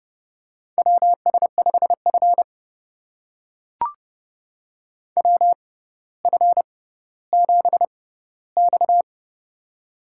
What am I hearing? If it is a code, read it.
Morse code, heard as a keyed tone, about 32 wpm, spaced as for 16 wpm, 700 Hz. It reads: WS5F WF7X
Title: USA Call Signs - 4 Characters - 6x Character Spacing - 30wpm